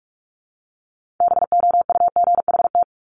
Morse code, heard as a keyed tone, about 46 wpm, 700 Hz. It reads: BOUGHT
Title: Multi-letter Phonemes Repeated 3x 45wpm